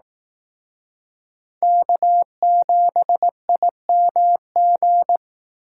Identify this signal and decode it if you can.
Morse code, heard as a keyed tone, about 18 wpm, 700 Hz. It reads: K7IMG
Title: Speed Racing - USA 1x3 Call Signs 15wpm